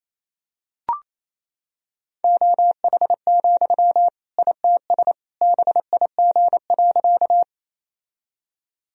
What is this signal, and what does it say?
Morse code, heard as a keyed tone, about 27 wpm, 700 Hz. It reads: OH, ITS BIG.
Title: Sentences from Top 1000 Words - Review Part 1 - 28wpm